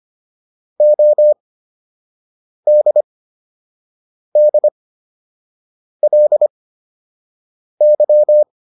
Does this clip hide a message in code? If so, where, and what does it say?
Yes: Morse code from 0.8 to 8.4 s, heard as a keyed tone, about 25 wpm, spaced as for 6 wpm, 600 Hz. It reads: ODDLY